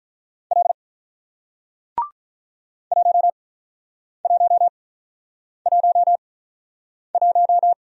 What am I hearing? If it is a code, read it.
Morse code, heard as a keyed tone, about 47 wpm, spaced as for 9 wpm, 700 Hz. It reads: L 1111